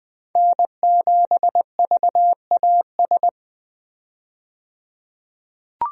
Morse code, heard as a keyed tone, about 21 wpm, 700 Hz. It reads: N7VAS
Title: USA Call Signs - 5 Characters - 20wpm